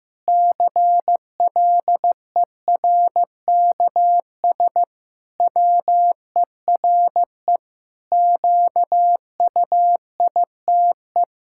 Morse code, heard as a keyed tone, about 15 wpm, 700 Hz. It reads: CLERKS WERE QUITE
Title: Pangram Challenge v2 - 15wpm